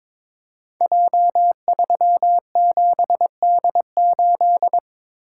Morse code, heard as a keyed tone, about 22 wpm, 700 Hz. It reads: J37D8
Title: Code Groups - 5 Letter:Numbers - 22wpm